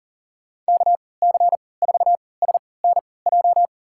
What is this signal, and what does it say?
Morse code, heard as a keyed tone, about 38 wpm, spaced as for 22 wpm, 700 Hz. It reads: KC4SNJ
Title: USA Call Signs - 6 Characters - 3x Character Spacing - 40wpm